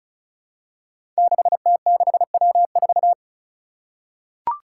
Morse code, heard as a keyed tone, about 34 wpm, 700 Hz. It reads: 6T6W4
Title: Five Challenging Letter-Number Combinations 35wpm